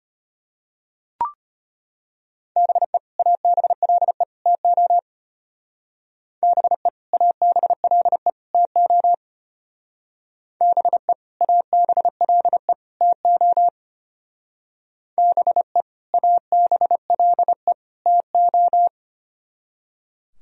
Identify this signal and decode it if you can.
Morse code, heard as a keyed tone, about 32 wpm, 700 Hz. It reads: BE ABLE TO BE ABLE TO BE ABLE TO BE ABLE TO